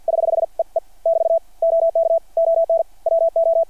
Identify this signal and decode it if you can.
Morse code, heard as a keyed tone, about 50 wpm, 660 Hz. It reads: <HH> E E <BT> OK ON WO
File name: archived / CRSnet